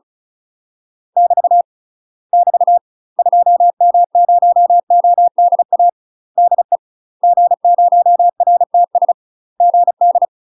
Morse code, heard as a keyed tone, about 34 wpm, 700 Hz. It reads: <BT> <BT> 2M0ODA DE G0RTS GD
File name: UK MASTER QSOs TRAINING / 35WPM / UK MASTER QSOs 0024